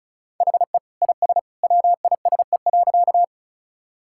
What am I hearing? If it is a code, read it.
Morse code, heard as a keyed tone, about 36 wpm, 700 Hz. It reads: HE IS WISE.
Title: Sentences from Top 1600 Words 35wpm